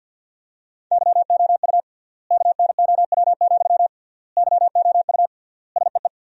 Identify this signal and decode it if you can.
Morse code, heard as a keyed tone, about 50 wpm, 700 Hz. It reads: YOU KNOW, YOU SEE
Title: 3x Word-Spacing - Sentences from Top 200 Words - 50wpm